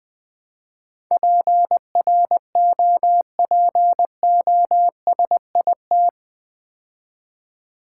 Morse code, heard as a keyed tone, about 20 wpm, 700 Hz. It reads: PROPOSIT
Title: Build A Word - 4 Letters At A Time - 11-Letter Words - 20wpm